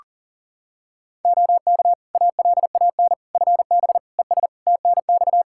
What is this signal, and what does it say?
Morse code, heard as a keyed tone, about 40 wpm, 700 Hz. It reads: OK ALAN FB ES TNX
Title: Practice QSO - Exchange 2 - Info - 40wpm